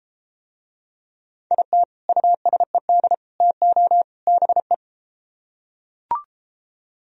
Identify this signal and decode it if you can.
Morse code, heard as a keyed tone, about 31 wpm, 700 Hz. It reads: IT USED TO BE